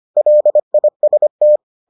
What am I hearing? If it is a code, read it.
Morse code, heard as a keyed tone, about 25 wpm, 600 Hz. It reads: LIST